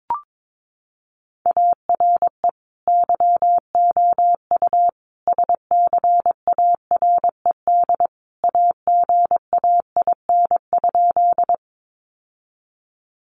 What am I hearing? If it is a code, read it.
Morse code, heard as a keyed tone, about 22 wpm, 700 Hz. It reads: ARE YOU SCARED AGAIN?
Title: Sentences from Top 700 Words 22wpm